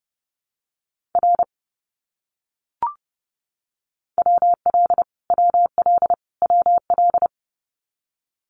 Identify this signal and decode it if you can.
Morse code, heard as a keyed tone, about 30 wpm, 700 Hz. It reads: R WL WL WL
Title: Abbreviations Repeated 3x - 30wpm